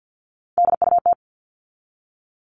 Morse code, heard as a keyed tone, about 50 wpm, 700 Hz. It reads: DUT
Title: All Cut Numbers - 3-Digit 50wpm